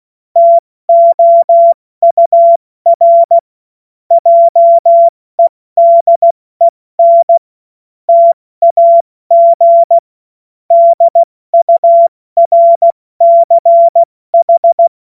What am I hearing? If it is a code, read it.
Morse code, heard as a keyed tone, about 16 wpm, 680 Hz. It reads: TOUR JEDEN TAG DURCH
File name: Morsekurs / Witze / Witz 31. Mai 2022 70 F80